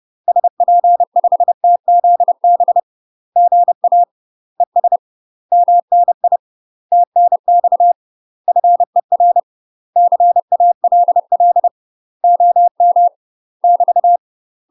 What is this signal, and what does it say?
Morse code, heard as a keyed tone, about 30 wpm, 700 Hz. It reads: SP5TZB GA ES MNI TNX FER CALL OM <BT>
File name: UK MASTER QSOs TRAINING / 30WPM / UK MASTER QSOs 0002